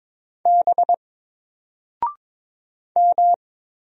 Morse code, heard as a keyed tone, about 22 wpm, 700 Hz. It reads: B M
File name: Single-Letter-Number-22wpm